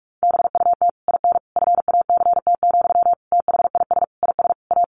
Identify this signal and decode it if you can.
Morse code, heard as a keyed tone, about 44 wpm, 700 Hz. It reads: BUT IN FACT, THIS IS A
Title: Sentences from Top 1500 Words 45wpm